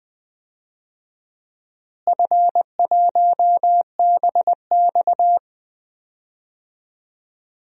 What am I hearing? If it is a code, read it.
Morse code, heard as a keyed tone, about 20 wpm, 700 Hz. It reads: F1BX
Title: DXCC-100 Call Signs - 4 Characters - Repeated 2x - 20wpm